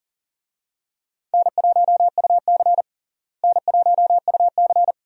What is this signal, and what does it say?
Morse code, heard as a keyed tone, about 40 wpm, 700 Hz. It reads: N1UC N1UC
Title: USA Call Signs - 4 Characters - Repeated 2x - 40wpm